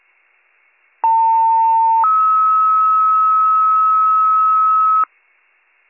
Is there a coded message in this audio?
Yes: two-tone page from 1.0 to 5.0 s, heard as a one-second tone, then a three-second tone, about 900 then 1290 Hz, with noise.